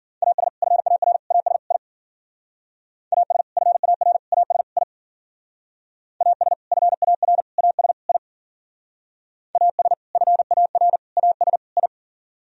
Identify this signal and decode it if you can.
Morse code, heard as a keyed tone, about 51 wpm, 700 Hz. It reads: AS FAR AS I AS FAR AS I AS FAR AS I AS FAR AS I